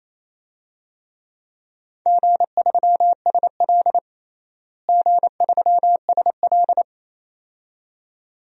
Morse code, heard as a keyed tone, about 28 wpm, 700 Hz. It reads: G3SL G3SL
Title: DXCC-100 Call Signs - 4 Characters - Repeated 2x - 28wpm